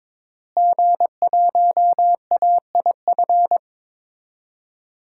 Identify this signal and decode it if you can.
Morse code, heard as a keyed tone, about 22 wpm, 700 Hz. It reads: G1AIF
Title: DXCC-100 Call Signs - 5 Characters - Repeated 3x - 22wpm